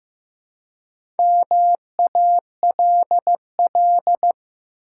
Morse code, heard as a keyed tone, about 16 wpm, 700 Hz. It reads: MALL
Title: Four Letter Words - 15wpm